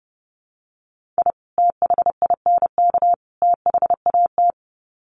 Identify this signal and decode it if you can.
Morse code, heard as a keyed tone, about 30 wpm, 700 Hz. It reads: I THINK THAT
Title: Sets of 3 Words 30wpm